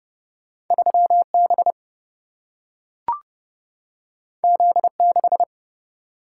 Morse code, heard as a keyed tone, about 30 wpm, 700 Hz. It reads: 3B Z6
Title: Single-Letter Single-Number 30wpm